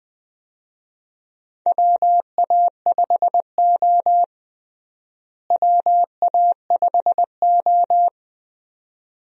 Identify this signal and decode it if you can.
Morse code, heard as a keyed tone, about 20 wpm, 700 Hz. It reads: WA5O WA5O
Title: USA Call Signs - 4 Characters - Repeated 2x - 20wpm